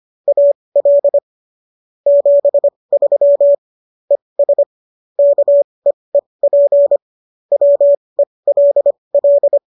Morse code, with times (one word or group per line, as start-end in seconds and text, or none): 0.3-1.2 AL
2.1-3.5 73
4.1-4.6 ES
5.2-7.0 KEEP
7.5-9.6 WELL